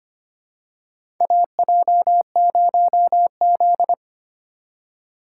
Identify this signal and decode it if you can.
Morse code, heard as a keyed tone, about 25 wpm, 700 Hz. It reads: AJ0Z
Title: USA Call Signs - 8x Character Spacing - 25wpm